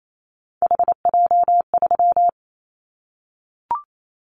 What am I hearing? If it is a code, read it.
Morse code, heard as a keyed tone, about 28 wpm, 700 Hz. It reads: HJ3
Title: Callsign Prefixes for Top 100 DXCC - 28wpm